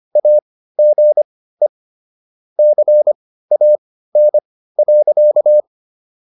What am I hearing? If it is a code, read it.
Morse code, heard as a keyed tone, about 25 wpm, spaced as for 15 wpm, 600 Hz. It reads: AGE CAN.